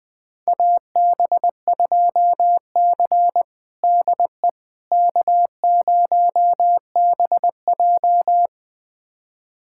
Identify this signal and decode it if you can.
Morse code, heard as a keyed tone, about 20 wpm, 700 Hz. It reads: AB2C DE K0BJ
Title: USA Callsign DE Callsign - 4 Characters - 20wpm